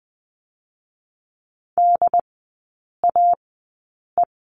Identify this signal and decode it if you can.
Morse code, heard as a keyed tone, about 20 wpm, spaced as for 8 wpm, 700 Hz. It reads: DAE
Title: 100 Lesson 17e - Letter u - Three Characters 20wpm